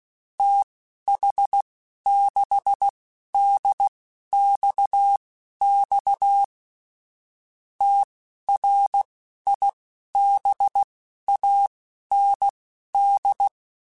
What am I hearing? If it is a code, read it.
Morse code, heard as a keyed tone, about 16 wpm, spaced as for 11 wpm, 800 Hz. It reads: TH6DXX TRIBAND